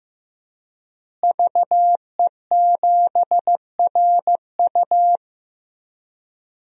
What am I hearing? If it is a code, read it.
Morse code, heard as a keyed tone, about 15 wpm, 700 Hz. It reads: VE7RU
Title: DXCC-100 Call Signs - 5 Characters - Repeated 2x - 15wpm